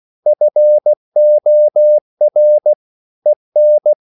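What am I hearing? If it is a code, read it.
Morse code, heard as a keyed tone, about 16 wpm, 600 Hz. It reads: FOR EN
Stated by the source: Slow (QRS) Morse code at 16 wpm, from CW med Gnister.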